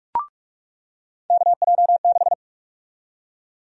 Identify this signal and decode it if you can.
Morse code, heard as a keyed tone, about 46 wpm, 700 Hz. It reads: KJ6
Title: USA Call Sign Prefixes - 45wpm